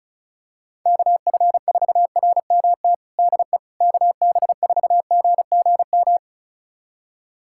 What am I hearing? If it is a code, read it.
Morse code, heard as a keyed tone, about 36 wpm, 700 Hz. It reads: KF4RMT DE KB4GGM
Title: USA Callsign DE Callsign - 6 Characters - 35wpm